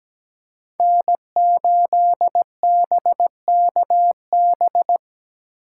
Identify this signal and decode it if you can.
Morse code, heard as a keyed tone, about 17 wpm, 700 Hz. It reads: N8BKB
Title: USA Call Signs - 5 Characters - 8x Character Spacing - 17wpm